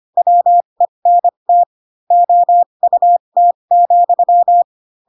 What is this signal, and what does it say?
Morse code, heard as a keyed tone, about 25 wpm, spaced as for 22 wpm, 700 Hz. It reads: WENT OUT,
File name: Pinocchio / Pinocchio ch7 18 25 ik1hge